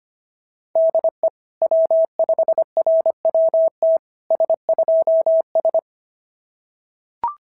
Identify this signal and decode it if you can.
Morse code, heard as a keyed tone, about 22 wpm, 650 Hz. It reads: DE W5RWT S2S
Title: SOTA QSO Practice 25wpm